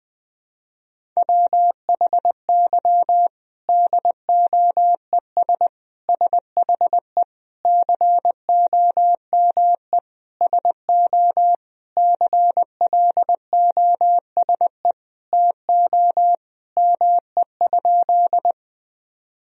Morse code, heard as a keyed tone, about 20 wpm, 700 Hz. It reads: WHY DOES SHE COME SO CLOSE TO ME?
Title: Sentences from Top 1000 Words - Review Part 1 - 20wpm